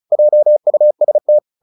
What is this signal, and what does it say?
Morse code, heard as a keyed tone, about 34 wpm, 600 Hz. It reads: JUST
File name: the old man and the sea 2 35wpm